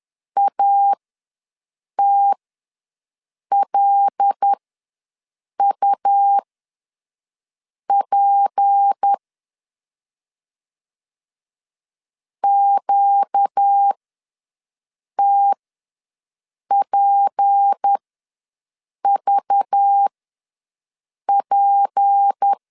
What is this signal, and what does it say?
Morse code, heard as a keyed tone, about 10 wpm, spaced as for 5 wpm, 800 Hz. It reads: ATLUP QTPVP
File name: LEZIONI / 6 PKX0 / 6 P ws 10wpm 3751